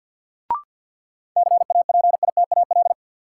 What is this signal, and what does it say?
Morse code, heard as a keyed tone, about 53 wpm, 700 Hz. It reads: CAPITAL
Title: Sentences from Top 1000 Words - New Word Drill - 50wpm